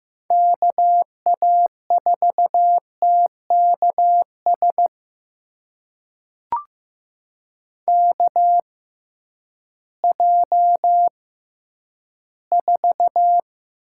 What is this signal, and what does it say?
Morse code, heard as a keyed tone, about 15 wpm, spaced as for 10 wpm, 700 Hz. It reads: KA4TKS KJ4
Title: USA Call Signs - 6 Characters - 6x Character Spacing - 15wpm